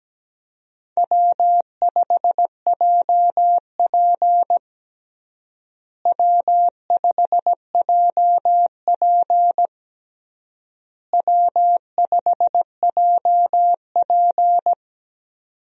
Morse code, heard as a keyed tone, about 17 wpm, 700 Hz. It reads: W5JP W5JP W5JP
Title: USA Call Signs - 4 Characters - Repeated 3x - 17wpm